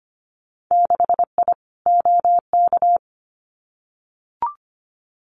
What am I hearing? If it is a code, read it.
Morse code, heard as a keyed tone, about 25 wpm, 690 Hz. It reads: 6I OK
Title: Winter Field Day Exchanges Encore 25wpm